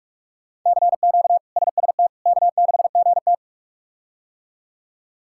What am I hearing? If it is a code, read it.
Morse code, heard as a keyed tone, about 46 wpm, 700 Hz. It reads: CQ SST K6GT
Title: K1USN SST QSO Practice - 45wpm